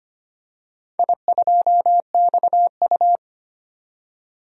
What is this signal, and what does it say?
Morse code, heard as a keyed tone, about 25 wpm, 700 Hz. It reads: I2XU
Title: DXCC-100 Call Signs - 4 Characters - 25wpm